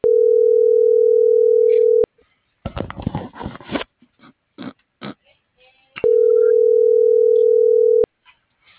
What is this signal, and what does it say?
Ringback tone (North America), 440 + 480 Hz.
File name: CENTRAL BLN / out hector.tellez